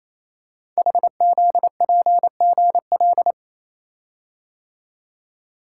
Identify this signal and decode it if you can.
Morse code, heard as a keyed tone, about 27 wpm, 700 Hz. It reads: HZPGL
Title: Code Groups - 5 Letters - 28wpm